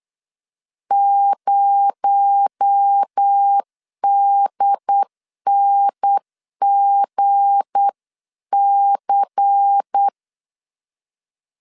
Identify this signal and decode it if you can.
Morse code, heard as a keyed tone, about 8 wpm, 800 Hz. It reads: 0DNGC
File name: LEZIONI / 8 YJ45 / 8 YJ45 8wpm 3214